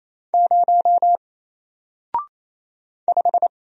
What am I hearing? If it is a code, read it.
Morse code, heard as a keyed tone, about 28 wpm, 700 Hz. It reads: 0 5
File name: Single-Letter-Number-28wpm